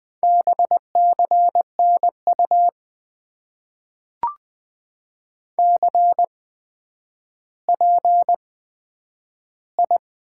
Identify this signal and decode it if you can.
Morse code, heard as a keyed tone, about 20 wpm, spaced as for 10 wpm, 700 Hz. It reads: BCNU CPI
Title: Top 90 QSO Elements - 8x Character Spacing - 20wpm